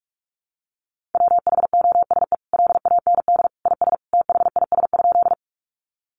Morse code, heard as a keyed tone, about 44 wpm, 700 Hz. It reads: WHOSE LAND IS THIS?
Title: Sentences from Top 1200 Words 45wpm